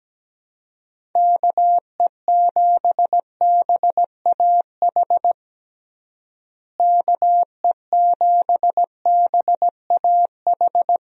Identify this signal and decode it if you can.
Morse code, heard as a keyed tone, about 17 wpm, 700 Hz. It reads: KE7BAH KE7BAH
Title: USA Call Signs - 6 Characters - Repeated 2x - 17wpm